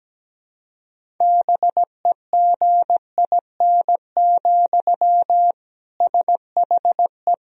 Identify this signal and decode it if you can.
Morse code, heard as a keyed tone, about 17 wpm, 700 Hz. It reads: BEGIN, SHE
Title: Sentences from Top 1000 Words - Review Part 1 - 17wpm